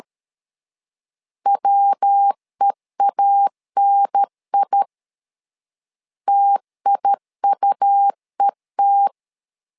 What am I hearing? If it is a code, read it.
Morse code, heard as a keyed tone, about 13 wpm, 800 Hz. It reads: WEANI TIUET
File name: LEZIONI / 3 MWRO / 3 MW 12wpm 1633